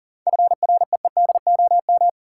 Morse code, heard as a keyed tone, about 40 wpm, 700 Hz. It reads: FREEDOM
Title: Sentences from Top 800 Words - New Word Drill - 40wpm